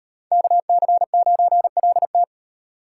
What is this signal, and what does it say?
Morse code, heard as a keyed tone, about 36 wpm, 700 Hz. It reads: KC9LT